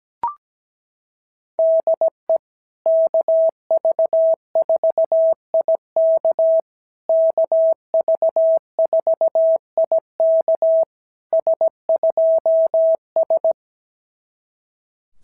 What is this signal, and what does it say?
Morse code, heard as a keyed tone, about 17 wpm, 650 Hz. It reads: DE KV4IK KV4IK S2S
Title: SOTA QSO Practice 17wpm